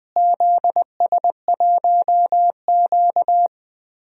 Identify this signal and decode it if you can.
Morse code, heard as a keyed tone, about 21 wpm, 700 Hz. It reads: ZS1Q
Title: DXCC-100 Call Signs - 4 Characters - Repeated 2x - 20wpm